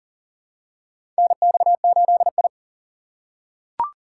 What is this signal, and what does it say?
Morse code, heard as a keyed tone, about 40 wpm, 700 Hz. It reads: NX8I
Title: USA Call Signs - 4 Characters - 3x Character Spacing - 40wpm